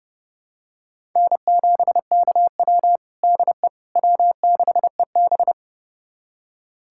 Morse code, heard as a keyed tone, about 30 wpm, 700 Hz. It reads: N7KW DE W6EB
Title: USA Callsign DE Callsign - 4 Characters - 30wpm